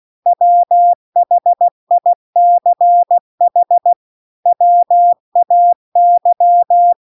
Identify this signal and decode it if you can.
Morse code, heard as a keyed tone, about 16 wpm, 700 Hz. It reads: WHICH WAY